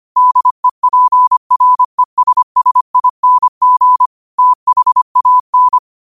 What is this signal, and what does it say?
Morse code, heard as a keyed tone, about 25 wpm, 1000 Hz. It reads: DEPRESSING THAN
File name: qotdfiles / 25wpm